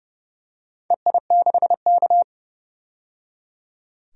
Morse code, heard as a keyed tone, about 30 wpm, 700 Hz. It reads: EI6K
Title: DXCC-100 Call Signs - 4 Characters - 30wpm